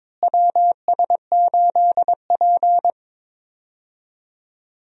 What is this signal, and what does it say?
Morse code, heard as a keyed tone, about 22 wpm, 700 Hz. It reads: WS8P
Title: USA Call Signs - 4 Characters - Repeated 2x - 22wpm